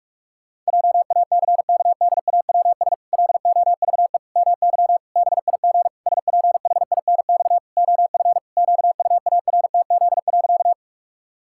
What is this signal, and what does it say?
Morse code, heard as a keyed tone, about 46 wpm, 700 Hz. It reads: JACKDAWS LOVE MY BIG SPHINX OF QUARTZ.
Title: Pangram Challenge v2 - 45wpm